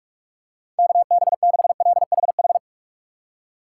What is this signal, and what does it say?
Morse code, heard as a keyed tone, about 46 wpm, 700 Hz. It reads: KB6LHH
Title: USA Call Signs - 6 Characters - 4x Character Spacing - 45wpm